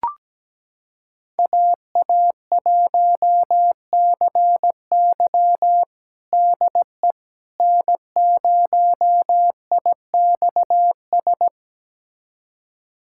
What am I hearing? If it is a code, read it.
Morse code, heard as a keyed tone, about 17 wpm, 700 Hz. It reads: AA1CY DE N0IXS
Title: USA Callsign DE Callsign - 5 Characters - 17wpm